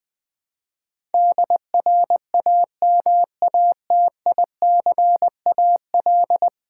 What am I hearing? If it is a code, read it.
Morse code, heard as a keyed tone, about 20 wpm, 700 Hz. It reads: DRAMATICAL
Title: Build A Word - 5 Letters At A Time - 12-Letter Words - 20wpm